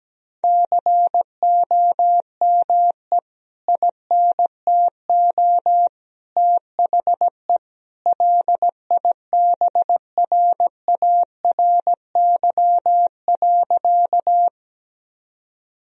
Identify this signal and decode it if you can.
Morse code, heard as a keyed tone, about 17 wpm, 700 Hz. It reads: COME INTO THE LIBRARY.
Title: Sentences from Top 1000 Words - Review Part 4 - 17wpm